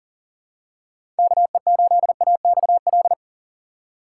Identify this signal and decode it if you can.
Morse code, heard as a keyed tone, about 40 wpm, 700 Hz. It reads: KE8AXL
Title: USA Call Signs - 3x Character Spacing - 40wpm